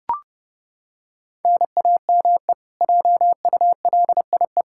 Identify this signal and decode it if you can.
Morse code, heard as a keyed tone, about 32 wpm, 700 Hz. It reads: NAME JULIE
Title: Practice QSO - Exchange 1 - Beginning - 30wpm